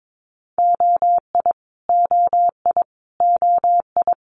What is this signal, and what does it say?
Morse code, heard as a keyed tone, about 22 wpm, 700 Hz. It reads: OI OI OI
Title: Multi-letter Phonemes Repeated 3x 22wpm